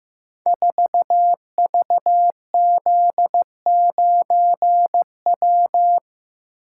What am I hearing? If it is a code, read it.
Morse code, heard as a keyed tone, about 15 wpm, 700 Hz. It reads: 4VZ9W
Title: Code Groups - 5 Letter:Numbers - 15wpm